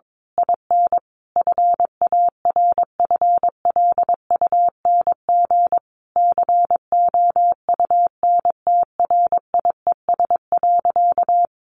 Morse code, heard as a keyed tone, about 22 wpm, 700 Hz. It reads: IN FARFLUNG COUNTRIES.
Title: A Sound Of Thunder 22wpm